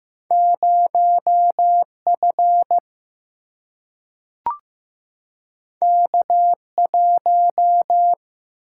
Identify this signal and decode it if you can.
Morse code, heard as a keyed tone, about 15 wpm, 700 Hz. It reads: 0 F K 1
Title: Single-Letter Single-Number 15wpm